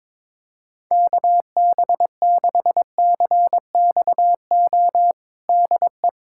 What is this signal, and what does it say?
Morse code, heard as a keyed tone, about 22 wpm, 700 Hz. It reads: KB6CXO DE
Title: USA Callsign DE Callsign - 6 Characters - 22wpm